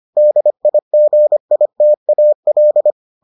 Morse code, heard as a keyed tone, about 25 wpm, 600 Hz. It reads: DIGITAL